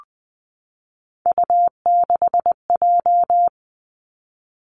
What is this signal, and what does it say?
Morse code, heard as a keyed tone, about 20 wpm, 700 Hz. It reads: U6J
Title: Three Challenging Letter-Number Combinations 20wpm